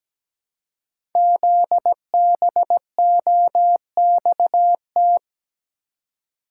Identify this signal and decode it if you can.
Morse code, heard as a keyed tone, about 17 wpm, 700 Hz. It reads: ZBOXT
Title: Code Groups - 5 Letters - 17wpm